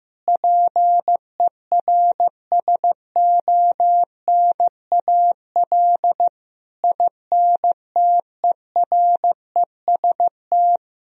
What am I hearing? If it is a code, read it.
Morse code, heard as a keyed tone, about 15 wpm, 700 Hz. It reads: PERSONAL INTEREST
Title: Sentences from Top 1000 Words - Review Part 4 - 15wpm